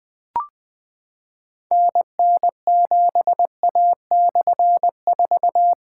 Morse code, heard as a keyed tone, about 20 wpm, 700 Hz. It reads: NN7A/4
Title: Difficult Call Signs 20wpm